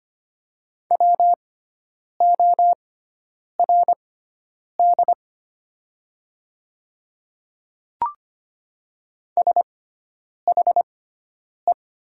Morse code, heard as a keyed tone, about 25 wpm, spaced as for 9 wpm, 700 Hz. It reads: WORD SHE
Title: 6x Character Spacing - Top 100 Words - 25wpm